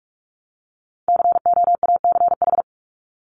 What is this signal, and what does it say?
Morse code, heard as a keyed tone, about 46 wpm, 700 Hz. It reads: COACH
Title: Top 1751-2000 Words 45wpm